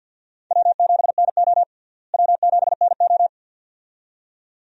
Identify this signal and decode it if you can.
Morse code, heard as a keyed tone, about 51 wpm, 700 Hz. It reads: W7NO W7NO
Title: USA Call Signs - 4 Characters - Repeated 2x - 50wpm